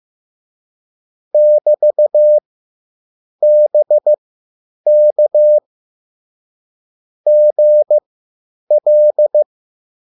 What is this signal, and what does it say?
Morse code, heard as a keyed tone, about 15 wpm, spaced as for 8 wpm, 600 Hz. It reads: <BT>BK GL
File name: QSL / CW lesson